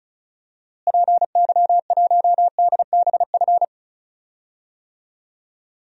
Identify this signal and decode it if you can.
Morse code, heard as a keyed tone, about 34 wpm, 700 Hz. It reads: PY1DBF
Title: DXCC-100 Call Signs - 6 Characters - Repeated 2x - 35wpm